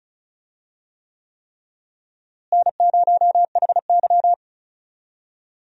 Morse code, heard as a keyed tone, about 36 wpm, 700 Hz. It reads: N0HY
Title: USA Call Signs - 4 Characters - Repeated 3x - 35wpm